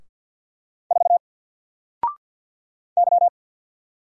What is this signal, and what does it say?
Morse code, heard as a keyed tone, about 48 wpm, 700 Hz. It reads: 4 Y
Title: Single Letter-Number Alt 50wpm